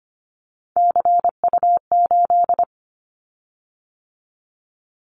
Morse code, heard as a keyed tone, about 25 wpm, 700 Hz. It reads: CU8
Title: Callsign Prefixes for Top 50 DXCC - 25wpm